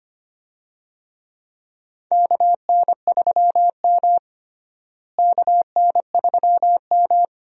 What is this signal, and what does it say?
Morse code, heard as a keyed tone, about 25 wpm, 700 Hz. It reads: KN3M KN3M
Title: USA Call Signs - 4 Characters - Repeated 2x - 25wpm